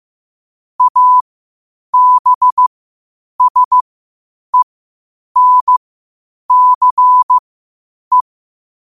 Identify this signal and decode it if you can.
Morse code, heard as a keyed tone, about 15 wpm, spaced as for 8 wpm, 1000 Hz. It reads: ABSENCE
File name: qotdfiles / 8wpm